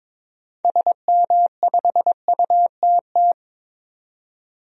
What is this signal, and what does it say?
Morse code, heard as a keyed tone, about 22 wpm, 700 Hz. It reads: SM5UTT
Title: DXCC-100 Call Signs - 6 Characters - 22wpm